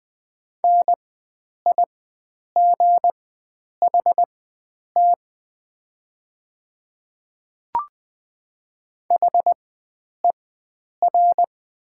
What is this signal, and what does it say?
Morse code, heard as a keyed tone, about 20 wpm, spaced as for 9 wpm, 700 Hz. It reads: NIGHT HER
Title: 4x Character Spacing - Top 300 Words - 20wpm